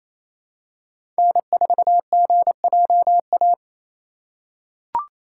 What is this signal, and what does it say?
Morse code, heard as a keyed tone, about 27 wpm, 700 Hz. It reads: N4GJA
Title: USA Call Signs - 5 Characters - 4x Character Spacing - 28wpm